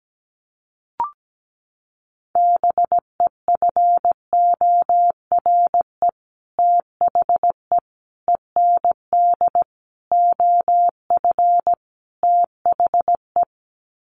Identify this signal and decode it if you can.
Morse code, heard as a keyed tone, about 17 wpm, 690 Hz. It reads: BEFORE THE END OF THE
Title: Sets of 5 Words Encore 17wpm